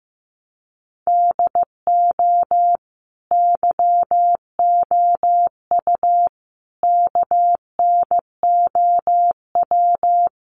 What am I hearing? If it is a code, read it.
Morse code, heard as a keyed tone, about 15 wpm, 700 Hz. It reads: DO YOU KNOW
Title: Sentences from Top 1400 Words 15wpm